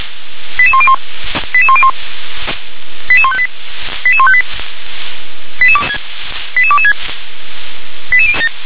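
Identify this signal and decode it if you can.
Five-tone selective call (ZVEI), heard as quick runs of five tones, tones of about 70 ms. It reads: 88101 88101 88158 88158 88226
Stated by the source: received on an SDR; ZVEI five-tone call 88277, 357 directory.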